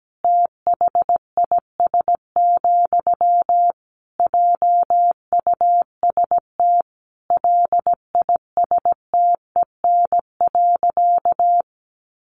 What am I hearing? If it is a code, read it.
Morse code, heard as a keyed tone, about 17 wpm, 700 Hz. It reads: THIS, JUST LISTEN.
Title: Sentences from Top 1300 Words 17wpm